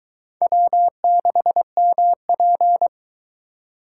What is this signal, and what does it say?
Morse code, heard as a keyed tone, about 21 wpm, 700 Hz. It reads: W6MP